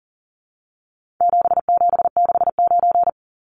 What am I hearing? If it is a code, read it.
Morse code, heard as a keyed tone, about 40 wpm, 700 Hz. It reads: 7769
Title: Numbers - 4-Digit 40wpm